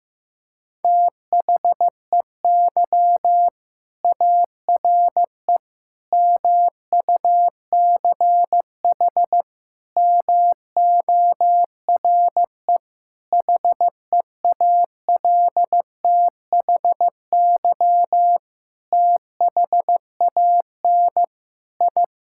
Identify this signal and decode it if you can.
Morse code, heard as a keyed tone, about 15 wpm, 700 Hz. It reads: THEY ARE MUCH MORE HEALTHY THAN I
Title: Sentences from Top 1000 Words - Review Part 5 - 15wpm